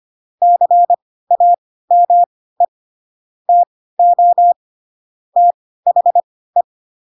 Morse code, heard as a keyed tone, about 25 wpm, spaced as for 16 wpm, 700 Hz. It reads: CAME TO THE
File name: Pinocchio / Pinocchio ch6 10 25 ik1hge